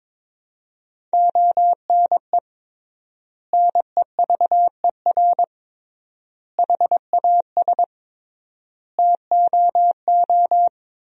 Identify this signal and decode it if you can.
Morse code, heard as a keyed tone, about 21 wpm, 700 Hz. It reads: ONE NEVER HAS TOO
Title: 3x Word-Spacing - Sentences from Top 300 Words - 22wpm